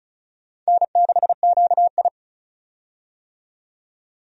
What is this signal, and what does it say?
Morse code, heard as a keyed tone, about 36 wpm, 700 Hz. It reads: N6QI
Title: USA Call Signs - 4 Characters - 35wpm